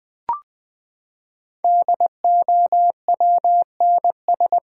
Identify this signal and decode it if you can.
Morse code, heard as a keyed tone, about 20 wpm, 700 Hz. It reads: DOWNS
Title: Build A Word - 5 Letters At A Time - 10-Letter Words - 20wpm